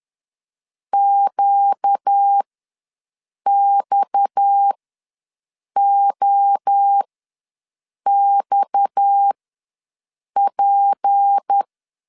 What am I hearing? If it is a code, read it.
Morse code, heard as a keyed tone, about 10 wpm, spaced as for 6 wpm, 800 Hz. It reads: QXOXP
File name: LEZIONI / 6 PKX0 / 6 PKX ws 10wpm 3706